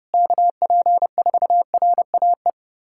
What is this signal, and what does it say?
Morse code, heard as a keyed tone, about 30 wpm, 700 Hz. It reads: KP4RAE
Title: USA Callsign DE Callsign - 6 Characters - 30wpm